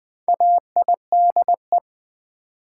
Morse code, heard as a keyed tone, about 20 wpm, 690 Hz. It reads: AIDE
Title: Build A Word - 2 Letters At A Time - 4-Letter Words - 20wpm